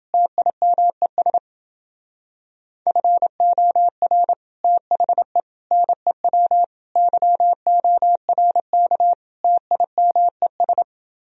Morse code, heard as a keyed tone, about 28 wpm, 700 Hz. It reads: TIMES FOR THE NEW YORK TIMES